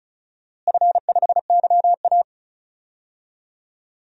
Morse code, heard as a keyed tone, about 34 wpm, 700 Hz. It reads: F5YA
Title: DXCC-100 Call Signs - 4 Characters - Repeated 3x - 35wpm